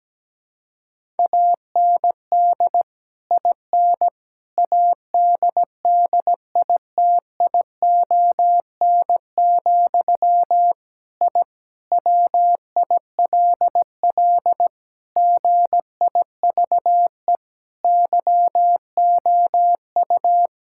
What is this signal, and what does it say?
Morse code, heard as a keyed tone, about 17 wpm, 700 Hz. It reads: AND IN ADDITION, I WILL GIVE YOU
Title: Sentences from Top 1000 Words - Review Part 6 - 17wpm